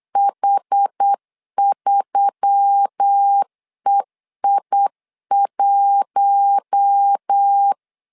Morse code, heard as a keyed tone, about 8 wpm, 800 Hz. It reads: H3EI1
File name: LEZIONI / 5 LBVC / 5 L 8wpm 2831